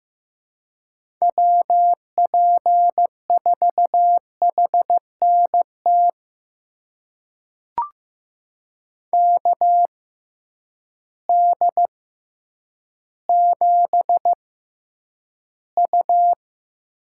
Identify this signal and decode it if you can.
Morse code, heard as a keyed tone, about 15 wpm, spaced as for 9 wpm, 700 Hz. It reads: WP4HNT KD7U
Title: USA Call Signs - 6 Characters - 6x Character Spacing - 15wpm